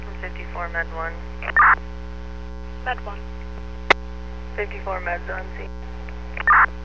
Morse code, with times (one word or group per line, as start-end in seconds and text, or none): none